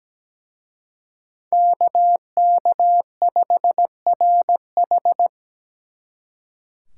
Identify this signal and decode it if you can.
Morse code, heard as a keyed tone, about 19 wpm, 700 Hz. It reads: KK5RH
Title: Speed Racing - USA 2x2 Call Signs 17wpm